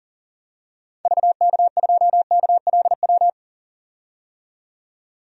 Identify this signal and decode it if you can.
Morse code, heard as a keyed tone, about 38 wpm, 700 Hz. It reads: VK2KLW
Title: DXCC-100 Call Signs - 6 Characters - 40wpm